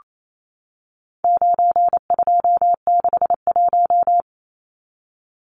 Morse code, heard as a keyed tone, about 28 wpm, 700 Hz. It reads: 9261
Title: Numbers - 4-Digit 28wpm